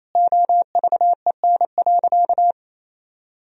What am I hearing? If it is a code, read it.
Morse code, heard as a keyed tone, about 28 wpm, 700 Hz. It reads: OVEN.
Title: Sentences from Top 1000 Words - Review Part 4 - 28wpm